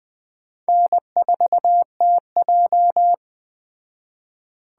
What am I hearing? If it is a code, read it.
Morse code, heard as a keyed tone, about 20 wpm, 700 Hz. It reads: N4TJ